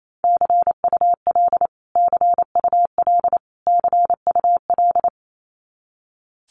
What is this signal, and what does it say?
Morse code, heard as a keyed tone, about 28 wpm, 700 Hz. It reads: CUL CUL CUL
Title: Abbreviations Repeated 3x - 28wpm